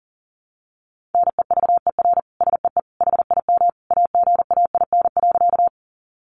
Morse code, heard as a keyed tone, about 40 wpm, 700 Hz. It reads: NEVER SEE HIM AGAIN.
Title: Sentences from Top 300 Words 40wpm